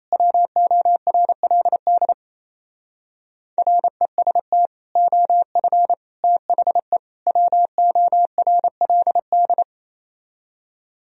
Morse code, heard as a keyed tone, about 36 wpm, 700 Hz. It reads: WORLD REST OF THE WORLD